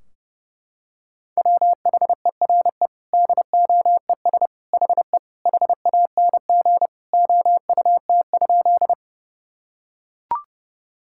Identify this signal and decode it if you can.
Morse code, heard as a keyed tone, about 30 wpm, 700 Hz. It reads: WHERE DOES HE HANG OUT?
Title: Sentences from Top 1600 Words 30wpm